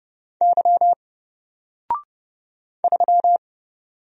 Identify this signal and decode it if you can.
Morse code, heard as a keyed tone, about 30 wpm, 700 Hz. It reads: Y 3
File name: Single-Letter-Number-30wpm